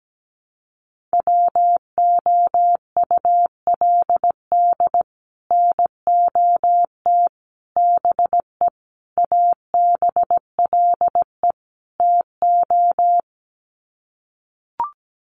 Morse code, heard as a keyed tone, about 17 wpm, 700 Hz. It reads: WOULD NOT BE ABLE TO
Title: Sets of 5 Words 17wpm